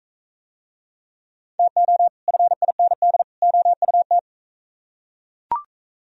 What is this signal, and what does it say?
Morse code, heard as a keyed tone, about 40 wpm, 700 Hz. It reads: TO FIND OUT